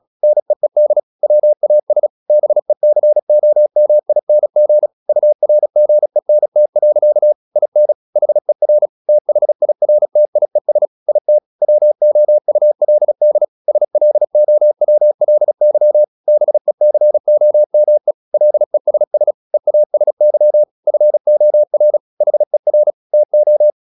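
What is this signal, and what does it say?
Morse code, heard as a keyed tone, about 36 wpm, 600 Hz. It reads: NEED WAS BECOMING URGENT. IN HER THIRTIES IT WOULD SLOWLY BECOME LESS EASY FOR HER TO
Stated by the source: Morse practice at 36 wpm, text of Ken Follett's novel Never, from CW med Gnister.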